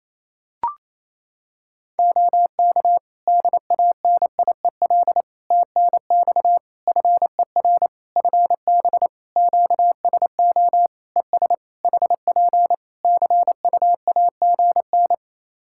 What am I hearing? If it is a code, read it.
Morse code, heard as a keyed tone, about 28 wpm, 700 Hz. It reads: OK DANIEL TNX FER FB QSO ES HP CUAGN
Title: Practice QSO - Exchange 4 - Ending - 28wpm